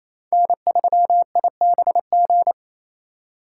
Morse code, heard as a keyed tone, about 27 wpm, 700 Hz. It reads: N3IBG
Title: USA Call Signs - 2x Character Spacing - 28wpm